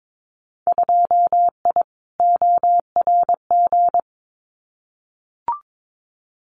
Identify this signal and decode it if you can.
Morse code, heard as a keyed tone, about 22 wpm, 700 Hz. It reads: 2I ORG
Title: Winter Field Day Exchanges Encore 22wpm